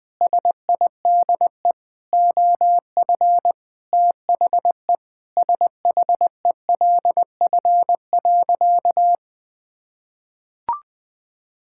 Morse code, heard as a keyed tone, about 20 wpm, 700 Hz. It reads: SIDE OF THE SHELF.
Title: Sentences with 7-Word Phrases - Spoken Beginning - 20wpm